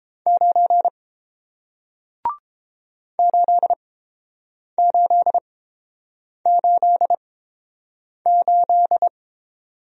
Morse code, heard as a keyed tone, about 30 wpm, spaced as for 7 wpm, 700 Hz. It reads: 9 8888